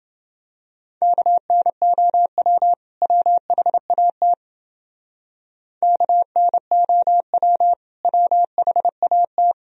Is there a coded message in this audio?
Yes: Morse code from 1.0 to 9.5 s, heard as a keyed tone, about 29 wpm, 700 Hz. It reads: KNOW WHAT KNOW WHAT